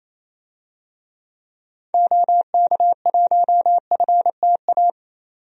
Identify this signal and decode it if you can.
Morse code, heard as a keyed tone, about 28 wpm, 700 Hz. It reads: OK1FTA
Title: DXCC-100 Call Signs - 6 Characters - 28wpm